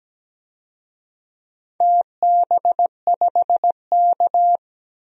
Morse code, heard as a keyed tone, about 17 wpm, 700 Hz. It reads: TB5K
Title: DXCC-100 Call Signs - 4 Characters - 17wpm